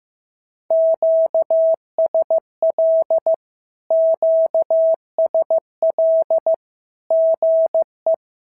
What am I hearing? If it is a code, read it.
Morse code, heard as a keyed tone, about 15 wpm, 650 Hz. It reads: QSL QSL GE
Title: SOTA QSO Practice 15wpm